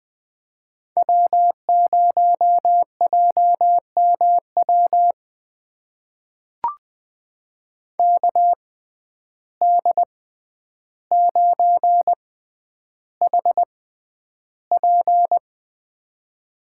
Morse code, heard as a keyed tone, about 20 wpm, spaced as for 10 wpm, 700 Hz. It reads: W0JMW KD9HP
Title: USA Call Signs - 6x Character Spacing - 20wpm